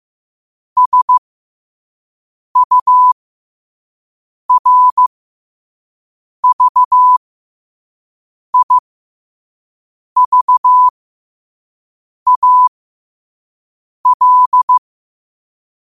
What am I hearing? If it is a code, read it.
Morse code, heard as a keyed tone, about 15 wpm, spaced as for 5 wpm, 1000 Hz. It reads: SURVIVAL
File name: qotdfiles / 5wpm